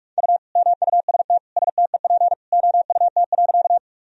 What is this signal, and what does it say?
Morse code, heard as a keyed tone, about 41 wpm, 700 Hz. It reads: U MUST STEP OUT.
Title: Sentences from Top 400 Words and Abbreviations 45wpm